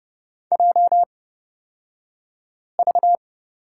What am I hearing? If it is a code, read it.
Morse code, heard as a keyed tone, about 30 wpm, spaced as for 5 wpm, 700 Hz. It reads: JV
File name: Lesson 36d - BT - Two Characters 30wpm